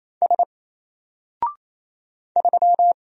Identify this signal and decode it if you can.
Morse code, heard as a keyed tone, about 27 wpm, 700 Hz. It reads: S 3
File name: Single-Letter-Number-28wpm